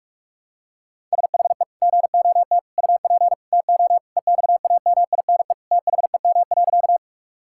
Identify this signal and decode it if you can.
Morse code, heard as a keyed tone, about 46 wpm, 700 Hz. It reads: SHE GOT UP TO EXAMINE THEM.
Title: Sentences from Top 1600 Words 45wpm